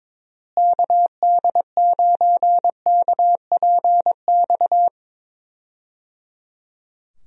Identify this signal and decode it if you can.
Morse code, heard as a keyed tone, about 22 wpm, 700 Hz. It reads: KD9KPX
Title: USA Call Signs - 6 Characters - Repeated 3x - 22wpm